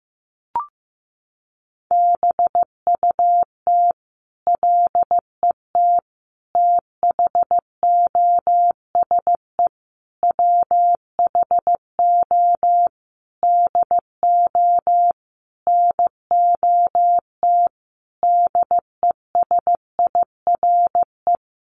Morse code, heard as a keyed tone, about 15 wpm, 700 Hz. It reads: BUT LET THOSE WHO DO NOT DESIRE
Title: Sentences from Top 1500 Words 15wpm